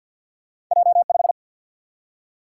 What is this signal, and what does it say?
Morse code, heard as a keyed tone, about 49 wpm, 700 Hz. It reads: J5
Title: Single-Letter Single-Number 50wpm